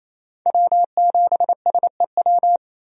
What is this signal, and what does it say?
Morse code, heard as a keyed tone, about 27 wpm, 700 Hz. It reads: W7SEW
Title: USA Call Signs - 5 Characters - 28wpm